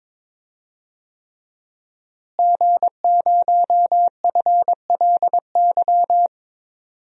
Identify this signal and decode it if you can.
Morse code, heard as a keyed tone, about 23 wpm, 700 Hz. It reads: G0FLY
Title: DXCC-100 Call Signs - 5 Characters - Repeated 3x - 22wpm